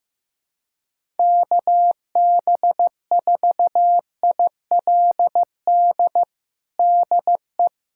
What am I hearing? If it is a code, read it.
Morse code, heard as a keyed tone, about 15 wpm, 700 Hz. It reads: KB4ILD DE
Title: USA Callsign DE Callsign - 6 Characters - 15wpm